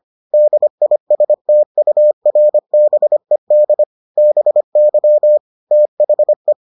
Morse code, heard as a keyed tone, about 25 wpm, 600 Hz. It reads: DISTURBED BY THE